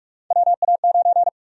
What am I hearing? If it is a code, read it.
Morse code, heard as a keyed tone, about 47 wpm, 700 Hz. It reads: WA9
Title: USA Call Sign Prefixes - 45wpm